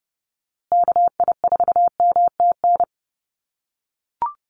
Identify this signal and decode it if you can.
Morse code, heard as a keyed tone, about 30 wpm, 700 Hz. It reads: KI4MTN
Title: USA Call Signs Repeated 2x 30wpm